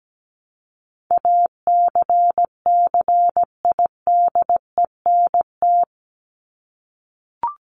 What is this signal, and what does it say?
Morse code, heard as a keyed tone, about 17 wpm, 700 Hz. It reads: ACCIDENT
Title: Top 1751-2000 Words 17wpm